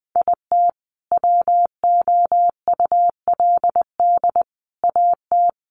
Morse code, heard as a keyed tone, about 20 wpm, 700 Hz. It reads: IT WOULD AT
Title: Sentences from Top 700 Words 20wpm